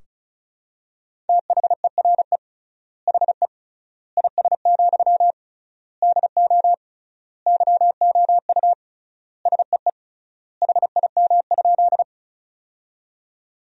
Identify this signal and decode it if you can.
Morse code, heard as a keyed tone, about 36 wpm, 700 Hz. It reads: THERE HE IS, DO YOU SEE HIM?
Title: 3x Word-Spacing - Sentences from Top 100 Words - 35wpm